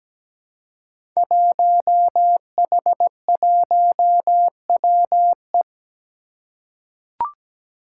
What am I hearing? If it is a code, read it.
Morse code, heard as a keyed tone, about 17 wpm, 700 Hz. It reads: 1H1WE
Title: Five Challenging Letter-Number Combinations 17wpm